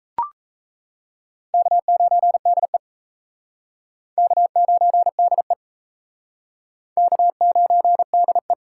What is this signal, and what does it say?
Morse code, heard as a keyed tone, about 38 wpm, 700 Hz. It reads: K9DE K9DE K9DE